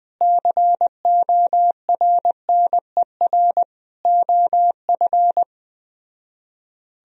Morse code, heard as a keyed tone, about 20 wpm, 700 Hz. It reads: CORNER OF
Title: Speed-Racing - Sets of 5 Words 17wpm